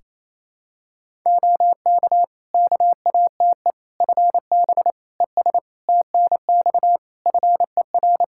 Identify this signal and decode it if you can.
Morse code, heard as a keyed tone, about 27 wpm, 700 Hz. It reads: OK KATE FB ES TNX FER
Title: Practice QSO - Exchange 2 - Info - 28wpm